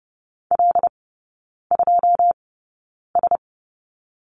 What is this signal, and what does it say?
Morse code, heard as a keyed tone, about 30 wpm, spaced as for 9 wpm, 700 Hz. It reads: L2S
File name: 081 Lesson 14e - Number 2 - Three Characters 30wpm